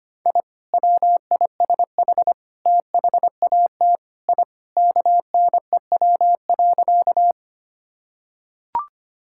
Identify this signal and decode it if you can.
Morse code, heard as a keyed tone, about 25 wpm, 700 Hz. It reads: I WISH THAT I KNEW.
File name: practice / 25wpm / Sentences with 2-Word Phrases - 25wpm